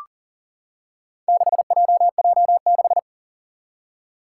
Morse code, heard as a keyed tone, about 40 wpm, 700 Hz. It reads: 6JJ6
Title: Four Challenging Letter-Number Combinations 40wpm